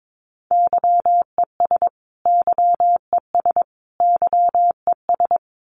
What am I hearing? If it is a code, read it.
Morse code, heard as a keyed tone, about 22 wpm, 700 Hz. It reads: YES YES YES
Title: Top 500 Words Repeated 22wpm